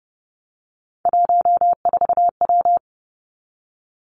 Morse code, heard as a keyed tone, about 30 wpm, 700 Hz. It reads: 14W
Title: Three Challenging Letter-Number Combinations 30wpm